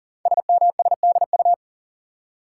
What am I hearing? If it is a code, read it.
Morse code, heard as a keyed tone, about 39 wpm, 690 Hz. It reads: SMSDU
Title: Code Groups - 5 Letters - 40wpm